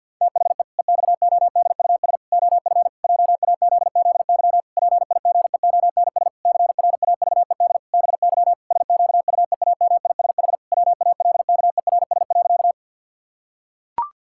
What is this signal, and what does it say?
Morse code, heard as a keyed tone, about 50 wpm, 700 Hz. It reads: THE EXODUS OF JAZZY PIGEONS CRAVED BY SQUEAMISH WALKERS.
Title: Pangram Challenge v2 - 50wpm